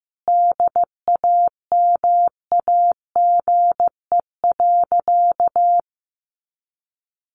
Morse code, heard as a keyed tone, about 15 wpm, 700 Hz. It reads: DAMAGE.
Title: Sentences from Top 1100 Words 15wpm